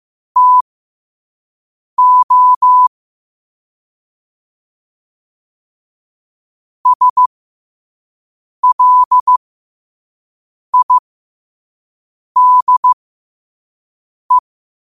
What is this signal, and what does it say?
Morse code, heard as a keyed tone, about 15 wpm, spaced as for 5 wpm, 1000 Hz. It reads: TO SLIDE
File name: qotdfiles / 5wpm